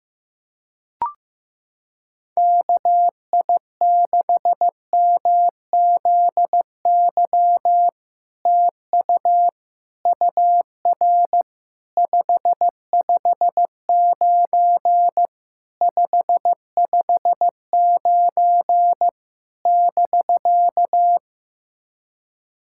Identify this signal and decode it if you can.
Morse code, heard as a keyed tone, about 15 wpm, 700 Hz. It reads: KI6MZY TU UR 559 559 <BK>
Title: SOTA QSO Practice 15wpm